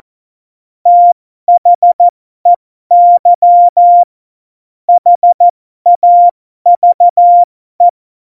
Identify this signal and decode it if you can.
Morse code, heard as a keyed tone, about 14 wpm, spaced as for 12 wpm, 700 Hz. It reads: THEY HAVE
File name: Morsekurs / Witze